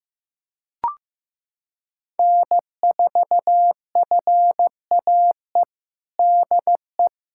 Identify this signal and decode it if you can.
Morse code, heard as a keyed tone, about 15 wpm, 700 Hz. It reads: N4FAE DE
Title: USA Callsign DE Callsign - 5 Characters - 15wpm